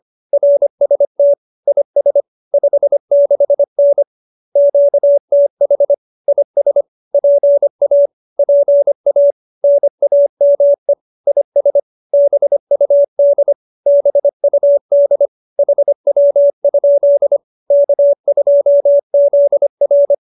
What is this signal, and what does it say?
Morse code, heard as a keyed tone, about 25 wpm, 570 Hz. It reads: RST IS 56N QTH IS PA PA NAME IS BUD BUD HW? K2ZR
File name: QSO302 25